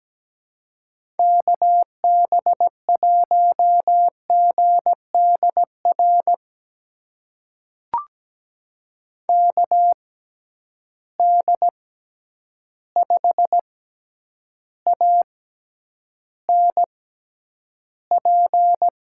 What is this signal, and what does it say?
Morse code, heard as a keyed tone, about 17 wpm, spaced as for 9 wpm, 700 Hz. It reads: KB1GDR KD5ANP
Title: USA Call Signs - 6 Characters - 6x Character Spacing - 17wpm